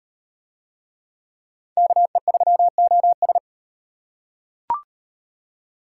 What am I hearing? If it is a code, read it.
Morse code, heard as a keyed tone, about 36 wpm, 700 Hz. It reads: KE3OS